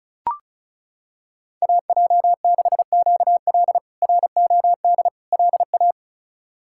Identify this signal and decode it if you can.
Morse code, heard as a keyed tone, about 34 wpm, 700 Hz. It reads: AJ6QL ROD LA
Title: K1USN SST QSO Practice - 35wpm